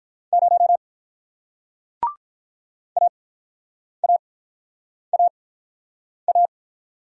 Morse code, heard as a keyed tone, about 48 wpm, spaced as for 9 wpm, 700 Hz. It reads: 0 AAAA